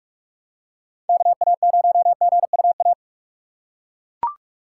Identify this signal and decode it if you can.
Morse code, heard as a keyed tone, about 42 wpm, 700 Hz. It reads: KA0GUA
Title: USA Call Signs - 6 Characters - 3x Character Spacing - 45wpm